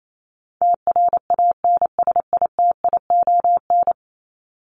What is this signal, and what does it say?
Morse code, heard as a keyed tone, about 28 wpm, 700 Hz. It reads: TRANSITION
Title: Top 1751-2000 Words 28wpm